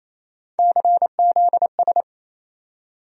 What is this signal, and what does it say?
Morse code, heard as a keyed tone, about 28 wpm, 700 Hz. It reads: CZS
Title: USA Call Sign Postfixes - 3 Letters 28wpm